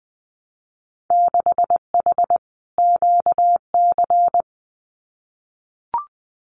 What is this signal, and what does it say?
Morse code, heard as a keyed tone, about 20 wpm, 700 Hz. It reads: 6H QC
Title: Winter Field Day Exchanges Encore 20wpm